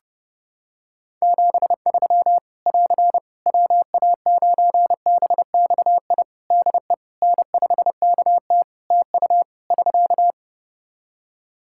Morse code, heard as a keyed tone, about 30 wpm, 700 Hz. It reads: 73 <AR> WA9BXI DE N5KT TU <SK>
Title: Practice QSO - Exchange 4 - Ending - 30wpm